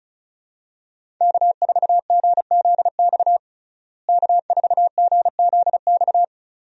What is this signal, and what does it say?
Morse code, heard as a keyed tone, about 34 wpm, 700 Hz. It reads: K4GZX K4GZX
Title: USA Call Signs - 5 Characters - Repeated 2x - 35wpm